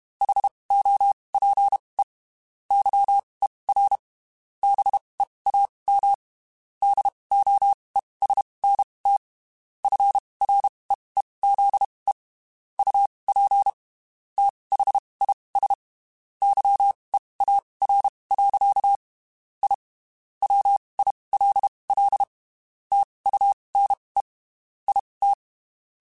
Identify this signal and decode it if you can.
Morse code, heard as a keyed tone, about 32 wpm, spaced as for 23 wpm, 800 Hz. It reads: HOPE YER BEAM DOESNT FREEZE UP THIS YEAR. I WILL TUNE IT
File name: (spaced) / QSOs Spaced-mp3s / WD0FBT @30 sp